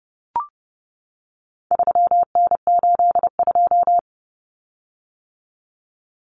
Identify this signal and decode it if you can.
Morse code, heard as a keyed tone, about 30 wpm, 700 Hz. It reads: 3N82
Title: Common Cut Numbers - 4-Digit 30wpm